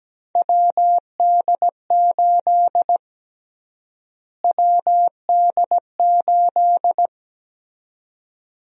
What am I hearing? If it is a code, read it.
Morse code, heard as a keyed tone, about 17 wpm, 700 Hz. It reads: WD8 WD8
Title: USA Call Sign Prefixes - Repeated 2x - 17wpm